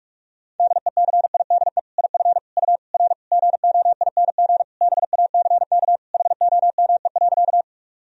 Morse code, heard as a keyed tone, about 41 wpm, 700 Hz. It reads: DECIDE IF U R GOING BACK HOME.
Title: Sentences from Top 500 Words and Abbreviations 45wpm